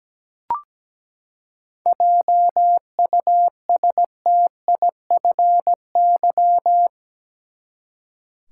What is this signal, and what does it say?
Morse code, heard as a keyed tone, about 17 wpm, 700 Hz. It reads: JUSTIFY
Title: Seven Letter Words - 17wpm